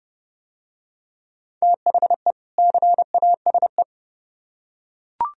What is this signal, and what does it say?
Morse code, heard as a keyed tone, about 28 wpm, 700 Hz. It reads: THE CASE